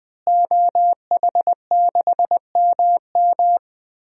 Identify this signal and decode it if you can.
Morse code, heard as a keyed tone, about 20 wpm, 700 Hz. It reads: OH6MM
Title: DXCC-100 Call Signs - 5 Characters - Repeated 3x - 20wpm